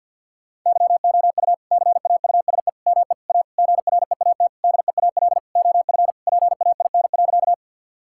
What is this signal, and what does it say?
Morse code, heard as a keyed tone, about 50 wpm, 700 Hz. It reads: YOU CAUSE ME A GREAT DEAL OF PAIN.
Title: Sentences from Top 1600 Words 50wpm